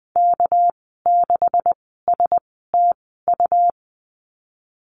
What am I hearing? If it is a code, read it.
Morse code, heard as a keyed tone, about 20 wpm, spaced as for 15 wpm, 700 Hz. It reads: K6STU
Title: 207 Lesson 33g - Letter j - Call Signs 20wpm